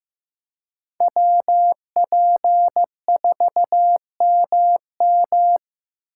Morse code, heard as a keyed tone, about 18 wpm, 700 Hz. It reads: WP4MM
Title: Speed Racing - USA 2x2 Call Signs 15wpm